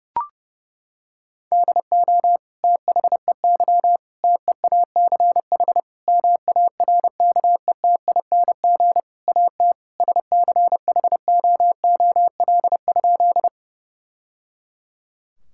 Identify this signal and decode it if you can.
Morse code, heard as a keyed tone, about 30 wpm, 700 Hz. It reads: DO THEY TEACH MARKETING AT SCHOOL?
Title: Sentences from Top 1000 Words - Review Part 5 - 30wpm